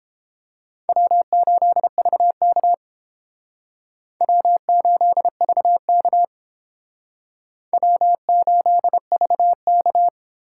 Morse code, heard as a keyed tone, about 29 wpm, 700 Hz. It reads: W8VK W8VK W8VK